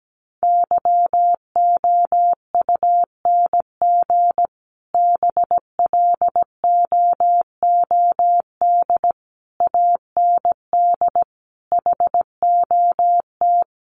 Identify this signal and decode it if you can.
Morse code, heard as a keyed tone, about 17 wpm, 700 Hz. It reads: YOUNG BLOOD AND HOT
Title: Sentences from Top 1100 Words 17wpm